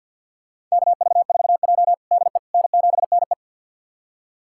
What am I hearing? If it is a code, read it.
Morse code, heard as a keyed tone, about 47 wpm, 700 Hz. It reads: KV4J DE N7NE
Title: USA Callsign DE Callsign - 4 Characters - 50wpm